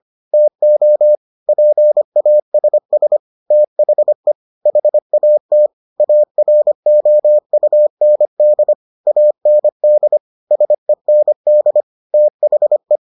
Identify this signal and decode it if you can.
Morse code, heard as a keyed tone, about 25 wpm, 600 Hz. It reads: TO PASS THE HAT AROUND AND SEND THE